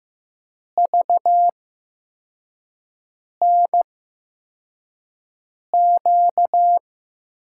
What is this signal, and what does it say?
Morse code, heard as a keyed tone, about 15 wpm, spaced as for 4 wpm, 700 Hz. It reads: VNQ
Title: USA Call Signs - 6 Characters - 8x Character Spacing - 15wpm